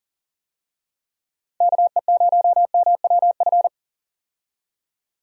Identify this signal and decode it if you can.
Morse code, heard as a keyed tone, about 38 wpm, 700 Hz. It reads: KE0MWF
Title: USA Call Signs - 6 Characters - 40wpm